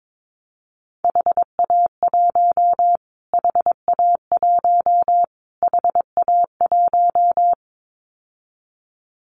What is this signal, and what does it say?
Morse code, heard as a keyed tone, about 22 wpm, 700 Hz. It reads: HA1 HA1 HA1
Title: Callsign Prefixes for Top 25 DXCC - Repeated 3x - 22wpm